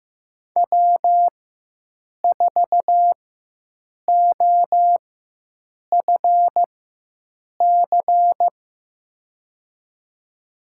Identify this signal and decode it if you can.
Morse code, heard as a keyed tone, about 15 wpm, spaced as for 7 wpm, 700 Hz. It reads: W4OFC
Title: USA Call Signs - 5 Characters - 4x Character Spacing - 15wpm